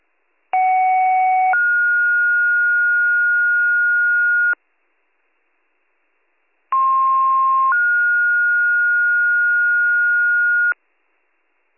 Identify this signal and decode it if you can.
Two-tone page, heard as a one-second tone, then a three-second tone, about 750 then 1430 Hz, about 1060 then 1430 Hz. It is heard from 0.5 to 10.7 s.